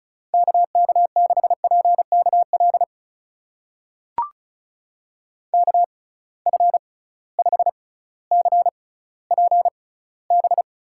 Morse code, heard as a keyed tone, about 36 wpm, spaced as for 18 wpm, 700 Hz. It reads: KK6PKL KF5CPB
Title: USA Call Signs - 6 Characters - 6x Character Spacing - 35wpm